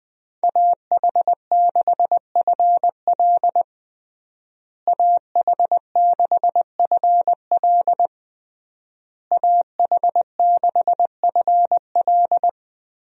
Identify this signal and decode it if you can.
Morse code, heard as a keyed tone, about 21 wpm, 700 Hz. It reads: AH6FL AH6FL AH6FL
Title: USA Call Signs - 5 Characters - Repeated 3x - 20wpm